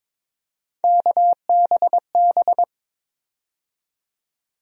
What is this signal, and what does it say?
Morse code, heard as a keyed tone, about 22 wpm, 700 Hz. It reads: KBB
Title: USA Call Sign Postfixes - 3 Letters 22wpm